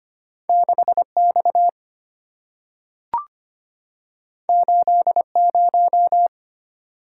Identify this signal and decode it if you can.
Morse code, heard as a keyed tone, about 25 wpm, 690 Hz. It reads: 6X 80
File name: Two Letter-Number Combinations - 25wpm